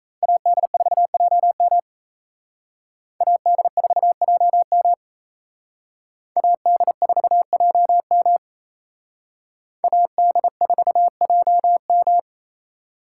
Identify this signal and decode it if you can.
Morse code, heard as a keyed tone, about 34 wpm, 700 Hz. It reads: AD4JM AD4JM AD4JM AD4JM
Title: Speed Racing - USA 2x2 Call Signs 28wpm